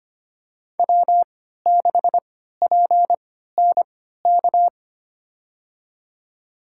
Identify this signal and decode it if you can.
Morse code, heard as a keyed tone, about 25 wpm, spaced as for 14 wpm, 700 Hz. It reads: W6PNK
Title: USA Call Signs - 3x Character Spacing - 25wpm